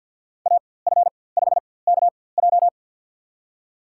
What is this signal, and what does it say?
Morse code, heard as a keyed tone, about 53 wpm, spaced as for 24 wpm, 700 Hz. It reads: AF5KJ
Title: USA Call Signs - 5 Characters - 4x Character Spacing - 50wpm